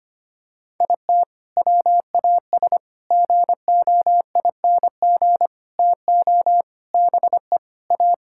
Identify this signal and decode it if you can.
Morse code, heard as a keyed tone, about 25 wpm, 700 Hz. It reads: IT WAS GOING TO BE A
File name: practice / 25wpm / Sentences with 4-Word Phrases - 25wpm